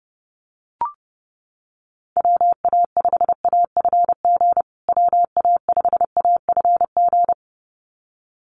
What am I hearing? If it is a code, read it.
Morse code, heard as a keyed tone, about 30 wpm, 700 Hz. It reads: WA5AFG WA5AFG
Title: USA Call Signs Repeated 2x 30wpm